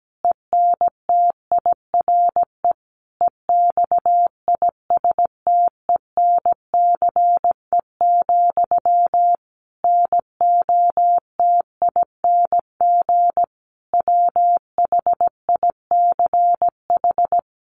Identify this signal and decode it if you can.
Morse code, heard as a keyed tone, about 17 wpm, 700 Hz. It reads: ENTIRE EXISTENCE, NOTING WHICH
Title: A Sound Of Thunder 17wpm